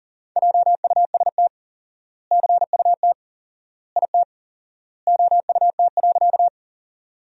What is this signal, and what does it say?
Morse code, heard as a keyed tone, about 37 wpm, 700 Hz. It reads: JUST CUT IT OUT.
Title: 4x Word-Spacing - Sentences from Top 300 Words - 40wpm